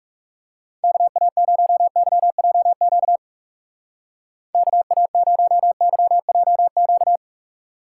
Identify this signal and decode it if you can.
Morse code, heard as a keyed tone, about 39 wpm, 700 Hz. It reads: KA0YJQ KA0YJQ